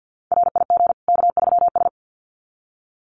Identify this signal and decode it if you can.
Morse code, heard as a keyed tone, about 50 wpm, 700 Hz. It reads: RIG K3S
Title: Common QSO Phrases 50wpm